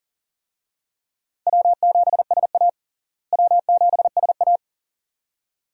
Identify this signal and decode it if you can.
Morse code, heard as a keyed tone, about 37 wpm, 700 Hz. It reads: W7SA W7SA
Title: USA Call Signs - 4 Characters - Repeated 2x - 40wpm